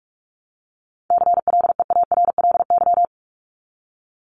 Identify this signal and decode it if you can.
Morse code, heard as a keyed tone, about 44 wpm, 700 Hz. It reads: CLEARLY
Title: Top 1001-1250 Words 45wpm